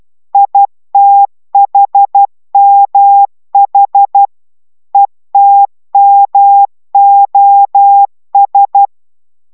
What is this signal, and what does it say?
Morse code, heard as a keyed tone, about 12 wpm, 800 Hz. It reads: ITHMH ETMOS